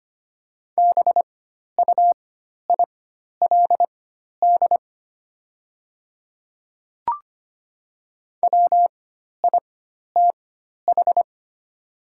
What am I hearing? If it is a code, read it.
Morse code, heard as a keyed tone, about 25 wpm, spaced as for 12 wpm, 700 Hz. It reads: BUILD WITH
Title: 4x Character Spacing - Top 300 Words - 25wpm